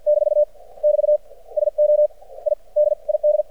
Morse code, heard as a keyed tone, about 51 wpm, spaced as for 42 wpm, 600 Hz. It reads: - <BT> SO I DID
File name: archived / CRSnet